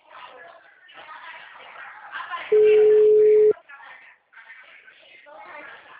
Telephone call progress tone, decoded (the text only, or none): ringback tone